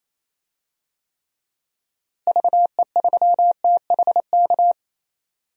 Morse code, heard as a keyed tone, about 28 wpm, 700 Hz. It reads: VE3THK
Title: DXCC-100 Call Signs - 6 Characters - Repeated 3x - 28wpm